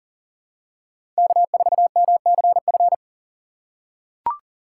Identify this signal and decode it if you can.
Morse code, heard as a keyed tone, about 40 wpm, 700 Hz. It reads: K4MCF
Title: USA Call Signs - 5 Characters - 3x Character Spacing - 40wpm